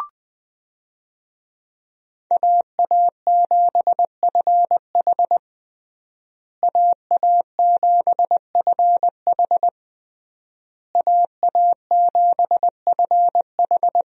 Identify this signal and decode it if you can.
Morse code, heard as a keyed tone, about 20 wpm, 700 Hz. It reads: AA7FH AA7FH AA7FH
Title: USA Call Signs - 5 Characters - Repeated 3x - 20wpm